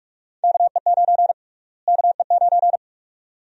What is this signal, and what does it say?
Morse code, heard as a keyed tone, about 46 wpm, 700 Hz. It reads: KE9 KE9
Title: USA Call Sign Prefixes - Repeated 2x - 45wpm